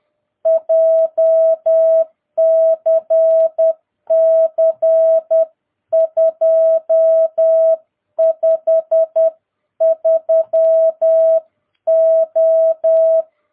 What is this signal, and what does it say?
Morse code, heard as a keyed tone, about 10 wpm, 650 Hz. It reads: JCC 253O
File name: jh8cbh / 2022cw